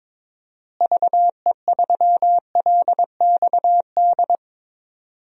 Morse code, heard as a keyed tone, about 22 wpm, 700 Hz. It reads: VE3LXD
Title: DXCC-100 Call Signs - 6 Characters - Repeated 3x - 22wpm